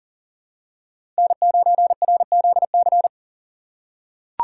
Morse code, heard as a keyed tone, about 40 wpm, 700 Hz. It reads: N9RZC
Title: USA Call Signs - 5 Characters - 4x Character Spacing - 40wpm